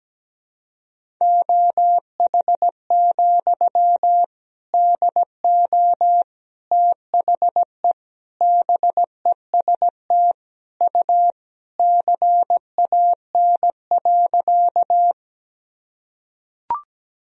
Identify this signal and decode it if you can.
Morse code, heard as a keyed tone, about 17 wpm, 700 Hz. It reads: OH, DO THE BEST U CAN.
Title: Sentences from Top 500 Words and Abbreviations 17wpm